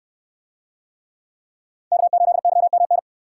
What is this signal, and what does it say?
Morse code, heard as a keyed tone, about 69 wpm, 700 Hz. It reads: K8YMN